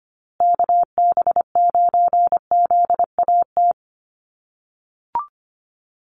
Morse code, heard as a keyed tone, about 25 wpm, 700 Hz. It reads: KB9ZAT
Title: USA Call Signs Repeated 3x 25wpm